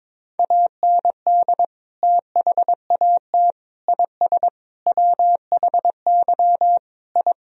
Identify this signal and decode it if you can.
Morse code, heard as a keyed tone, about 22 wpm, 700 Hz. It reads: AND THAT IS WHY I
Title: Sentences from Top 1000 Words - Review Part 3 - 22wpm